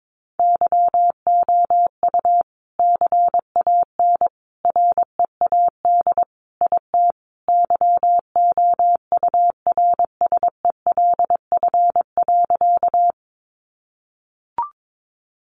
Morse code, heard as a keyed tone, about 22 wpm, 700 Hz. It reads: YOU CAN READ IT YOURSELF.
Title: Sentences from Top 600 Words 22wpm